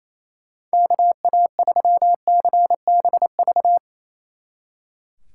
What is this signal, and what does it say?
Morse code, heard as a keyed tone, about 35 wpm, 700 Hz. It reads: KA3CBV
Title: Speed Racing - USA 2x3 Call Signs 28wpm